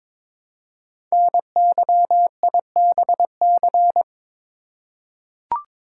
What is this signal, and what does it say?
Morse code, heard as a keyed tone, about 21 wpm, 700 Hz. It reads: NYIBC
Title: Code Groups - 5 Letter:Numbers - 22wpm